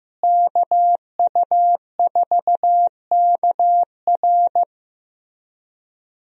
Morse code, heard as a keyed tone, about 15 wpm, 700 Hz. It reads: KU4KR
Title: USA Call Signs - 5 Characters - Repeated 3x - 15wpm